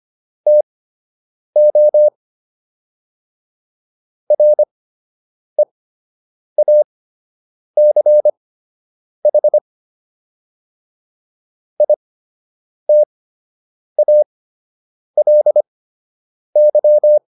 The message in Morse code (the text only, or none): TO REACH ITALY